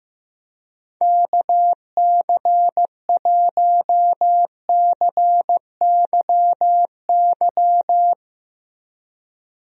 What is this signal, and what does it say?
Morse code, heard as a keyed tone, about 15 wpm, 700 Hz. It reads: KC1CYY
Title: USA Call Signs - 6 Characters - Repeated 2x - 15wpm